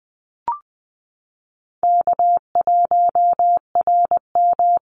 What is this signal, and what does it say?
Morse code, heard as a keyed tone, about 20 wpm, 700 Hz. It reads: K1RM
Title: Easy Call Signs - 20wpm